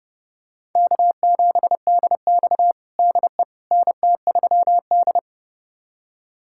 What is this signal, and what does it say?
Morse code, heard as a keyed tone, about 30 wpm, 700 Hz. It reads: K7DX DE NT3D
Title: USA Callsign DE Callsign - 4 Characters - 30wpm